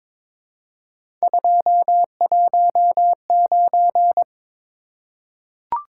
Morse code, heard as a keyed tone, about 23 wpm, 700 Hz. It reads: 219
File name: practice / 22wpm / Three Letter-Number Combinations - 22wpm